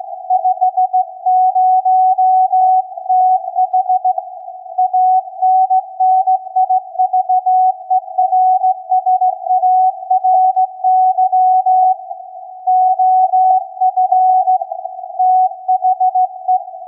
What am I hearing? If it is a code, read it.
Morse code, heard as a keyed tone, about 16 wpm, 740 Hz. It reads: 50TH ANNIVERSARY OF THE
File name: iz7slz / VLF / SAQ 16-11-2022